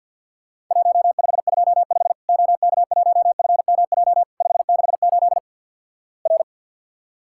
Morse code, heard as a keyed tone, about 50 wpm, 700 Hz. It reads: 1525 OK1FMJ 568 R
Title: Worked All Europe DX Contest - QTC Exchange - 50wpm